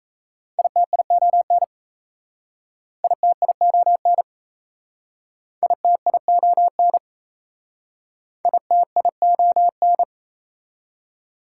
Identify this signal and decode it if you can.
Morse code, heard as a keyed tone, about 37 wpm, 700 Hz. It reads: ITION ITION ITION ITION